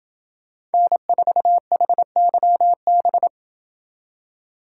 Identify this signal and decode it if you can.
Morse code, heard as a keyed tone, about 28 wpm, 700 Hz. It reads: N4HYB